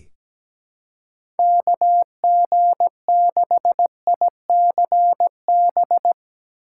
Morse code, heard as a keyed tone, about 17 wpm, 700 Hz. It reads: KG6ICB
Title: USA Call Signs - 6 Characters - 2x Character Spacing - 17wpm